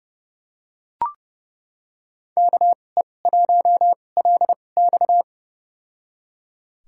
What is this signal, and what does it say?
Morse code, heard as a keyed tone, about 30 wpm, spaced as for 22 wpm, 700 Hz. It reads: KE1LX
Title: USA Call Signs - 5 Characters - 2x Character Spacing - 30wpm